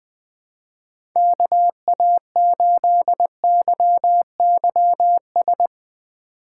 Morse code, heard as a keyed tone, about 20 wpm, 700 Hz. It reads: KA8YYS